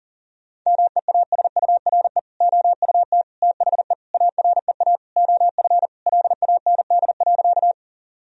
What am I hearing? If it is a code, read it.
Morse code, heard as a keyed tone, about 40 wpm, 700 Hz. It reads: MEASURE OUT THE AREA OF LAND.
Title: Sentences from Top 1000 Words - Review Part 2 - 40wpm